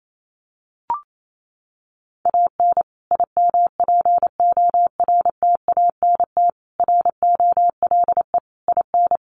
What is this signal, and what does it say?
Morse code, heard as a keyed tone, about 28 wpm, 700 Hz. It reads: AN IMPORTANT ROLE IN
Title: Sets of 5 Words Encore2 28wpm